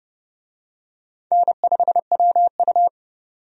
Morse code, heard as a keyed tone, about 31 wpm, 700 Hz. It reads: N5WU
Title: USA Call Signs - 4 Characters - 30wpm